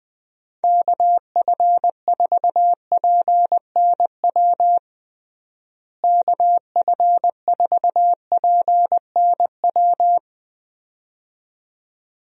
Morse code, heard as a keyed tone, about 20 wpm, 700 Hz. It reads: KF4PNW KF4PNW
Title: USA Call Signs - 6 Characters - Repeated 2x - 20wpm